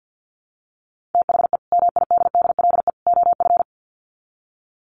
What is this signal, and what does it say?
Morse code, heard as a keyed tone, about 50 wpm, 700 Hz. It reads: THE MIDDLE OF
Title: Sets of 3 Words 50wpm